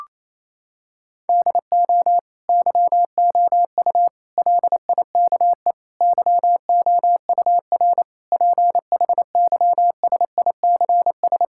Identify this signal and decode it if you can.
Morse code, heard as a keyed tone, about 28 wpm, 700 Hz. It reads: DO YOU LIKE YOUR PHYSICS
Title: Sentences from Top 1000 Words - Review Part 5 - 28wpm